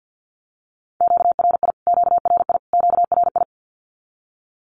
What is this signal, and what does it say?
Morse code, heard as a keyed tone, about 48 wpm, 700 Hz. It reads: QRI QRI QRI
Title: Q-Codes Repeated 3x 50wpm